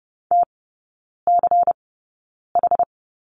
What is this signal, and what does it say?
Morse code, heard as a keyed tone, about 30 wpm, spaced as for 9 wpm, 700 Hz. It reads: TCH
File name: Lesson 16e - Letter c - Three Characters 30wpm